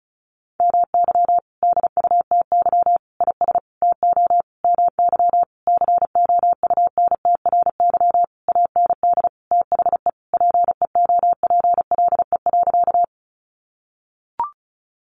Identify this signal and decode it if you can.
Morse code, heard as a keyed tone, about 34 wpm, 700 Hz. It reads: MY DUTY IS TO MY COUNTRY AND THE PEOPLE.
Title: Sentences from Top 1500 Words 35wpm